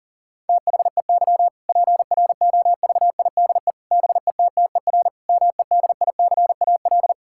Morse code, heard as a keyed tone, about 40 wpm, 700 Hz. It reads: THEY PROVIDE BETTER MEDICAL
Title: Sentences from Top 1000 Words - Review Part 5 - 40wpm